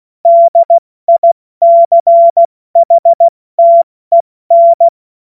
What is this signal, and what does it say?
Morse code, heard as a keyed tone, about 16 wpm, spaced as for 14 wpm, 680 Hz. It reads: DICHTEN
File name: Morsekurs / Witze / Witz 16. August 70 F80